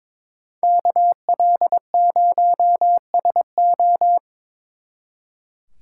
Morse code, heard as a keyed tone, about 24 wpm, 700 Hz. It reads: KL0SO
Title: Speed Racing - USA 2x2 Call Signs 22wpm